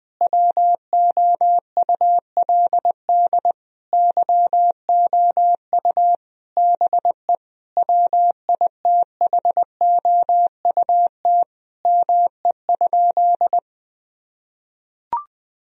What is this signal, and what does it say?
Morse code, heard as a keyed tone, about 21 wpm, 700 Hz. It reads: WOULD YOU BE WITHOUT ME?
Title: Sentences from Top 1000 Words - Review Part 3 - 20wpm